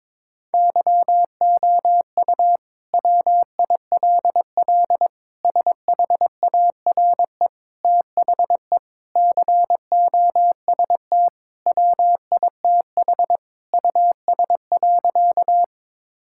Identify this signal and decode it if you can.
Morse code, heard as a keyed tone, about 22 wpm, 700 Hz. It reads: YOU WILL SHARE THE COST WITH US.
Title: Sentences from Top 1000 Words - Review Part 6 - 22wpm